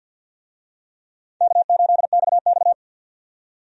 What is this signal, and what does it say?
Morse code, heard as a keyed tone, about 50 wpm, 700 Hz. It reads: K8XX
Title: USA Call Signs - 4 Characters - 50wpm